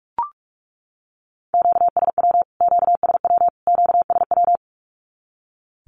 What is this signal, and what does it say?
Morse code, heard as a keyed tone, about 44 wpm, 700 Hz. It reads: QSW QSW QSW
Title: Q-Codes Repeated 3x 45wpm